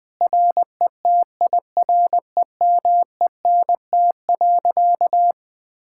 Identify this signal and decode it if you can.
Morse code, heard as a keyed tone, about 20 wpm, 700 Hz. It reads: RETIREMENT.
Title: Difficult Sentences with Hints - 20wpm